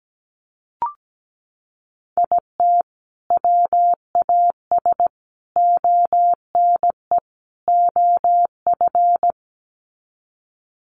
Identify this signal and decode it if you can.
Morse code, heard as a keyed tone, about 17 wpm, 700 Hz. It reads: IT WAS ONE OF
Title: Sets of 4 Words 17wpm